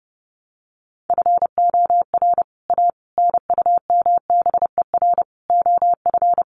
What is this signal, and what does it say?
Morse code, heard as a keyed tone, about 30 wpm, 700 Hz. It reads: FOR A NUMBER OF
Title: Sets of 5 Words Encore2 30wpm